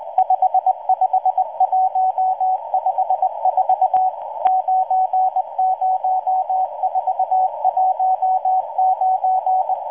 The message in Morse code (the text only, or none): 55154 90419